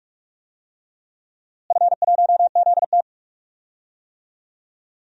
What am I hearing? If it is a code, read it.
Morse code, heard as a keyed tone, about 41 wpm, 700 Hz. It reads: F1ZT
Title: DXCC-100 Call Signs - 4 Characters - Repeated 2x - 45wpm